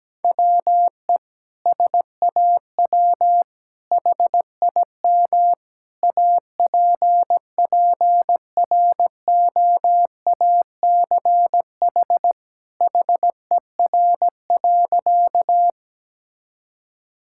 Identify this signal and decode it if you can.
Morse code, heard as a keyed tone, about 17 wpm, 700 Hz. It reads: WE SAW HIM APPROACH HER.
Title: Sentences from Top 1000 Words - Review Part 6 - 17wpm